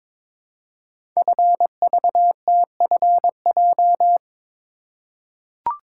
Morse code, heard as a keyed tone, about 23 wpm, 700 Hz. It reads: FVTFJ
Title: Code Groups - 5 Letters - 22wpm